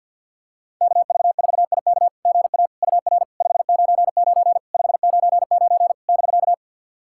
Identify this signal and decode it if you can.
Morse code, heard as a keyed tone, about 49 wpm, 700 Hz. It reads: KV4IK GA UR 599 599 <BK>
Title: SOTA QSO Practice 50wpm